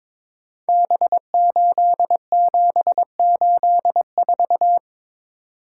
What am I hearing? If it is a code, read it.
Morse code, heard as a keyed tone, about 22 wpm, 700 Hz. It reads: B8784
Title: Code Groups - 5 Letter:Numbers - 22wpm